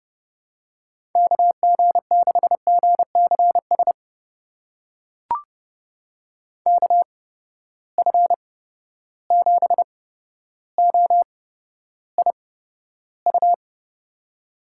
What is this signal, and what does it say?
Morse code, heard as a keyed tone, about 30 wpm, spaced as for 13 wpm, 700 Hz. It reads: KG6GCS KF7OIU
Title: USA Call Signs - 6 Characters - 8x Character Spacing - 30wpm